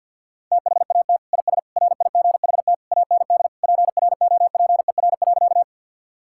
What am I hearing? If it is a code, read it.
Morse code, heard as a keyed tone, about 50 wpm, 700 Hz. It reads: THAT IS RIGHT AND PROPER.
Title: Sentences from Top 1000 Words - Review Part 5 - 50wpm